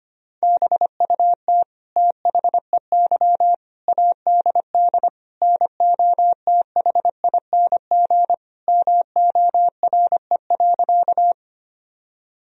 Morse code, heard as a keyed tone, about 25 wpm, 700 Hz. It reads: BUT THEY ADD NOTHING MORE.
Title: Sentences from Top 1000 Words - Review Part 1 - 25wpm